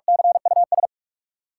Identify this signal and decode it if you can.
Morse code, heard as a keyed tone, about 47 wpm, 700 Hz. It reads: CUS
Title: Sentences from Top 1000 Words - Review Part 4 - New Word Drill - 45wpm